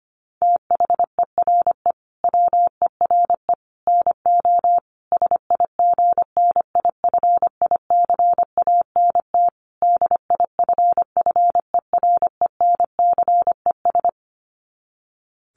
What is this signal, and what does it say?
Morse code, heard as a keyed tone, about 25 wpm, 700 Hz. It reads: THERE WERE NO SIGNIFICANT DIFFERENCES
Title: Sets of 5 Words Encore2 25wpm